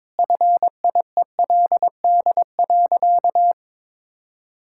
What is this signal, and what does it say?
Morse code, heard as a keyed tone, about 22 wpm, 700 Hz. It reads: FIELD.
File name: practice / 22wpm / Sentences with 4-Word Phrases - 22wpm